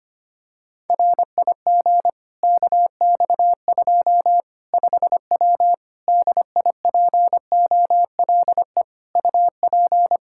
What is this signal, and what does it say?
Morse code, heard as a keyed tone, about 25 wpm, 700 Hz. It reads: RIG KX2 5W DIPOLE UP
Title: Practice QSO - Exchange 2 - Info - 25wpm